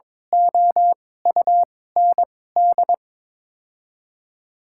Morse code, heard as a keyed tone, about 22 wpm, spaced as for 16 wpm, 700 Hz. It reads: OUND
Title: 2x Character Spacing - Top 500 Words - 22wpm